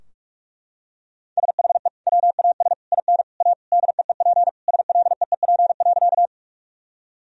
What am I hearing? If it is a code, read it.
Morse code, heard as a keyed tone, about 41 wpm, 700 Hz. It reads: SHE WAS IN A DEEP SLEEP.
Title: Sentences from Top 1000 Words - Review Part 3 - 45wpm